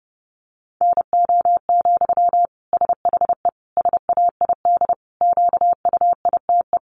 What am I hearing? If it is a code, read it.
Morse code, heard as a keyed tone, about 30 wpm, 700 Hz. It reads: NO, SHE SAID QUITE
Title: Sentences from Top 800 Words 30wpm